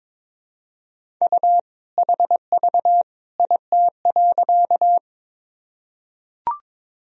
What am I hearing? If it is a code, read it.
Morse code, heard as a keyed tone, about 22 wpm, 700 Hz. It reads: U HV IT.
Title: Sentences from Top 100 Words and Abbreviations 22wpm